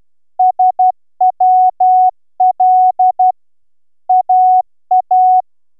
Morse code, heard as a keyed tone, about 11 wpm, 740 Hz. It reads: SWL AA